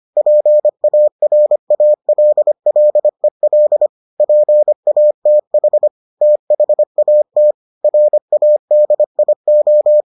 Morse code, heard as a keyed tone, about 25 wpm, 600 Hz. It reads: PARALLEL PATH THAT RADIO